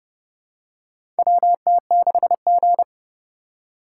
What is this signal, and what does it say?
Morse code, heard as a keyed tone, about 30 wpm, 700 Hz. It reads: WT6G
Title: USA Call Signs - 4 Characters - 2x Character Spacing - 30wpm